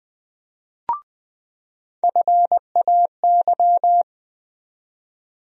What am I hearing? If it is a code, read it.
Morse code, heard as a keyed tone, about 20 wpm, 700 Hz. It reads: FAY
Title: USA Call Sign Postfixes - 3 Letters 20wpm